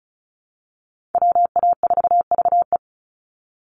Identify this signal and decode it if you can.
Morse code, heard as a keyed tone, about 34 wpm, 700 Hz. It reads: WA4VE
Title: USA Call Signs 35wpm